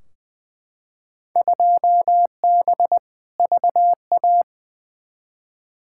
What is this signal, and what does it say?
Morse code, heard as a keyed tone, about 21 wpm, 700 Hz. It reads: 2B VA
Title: Field Day Exchanges Encore - 20wpm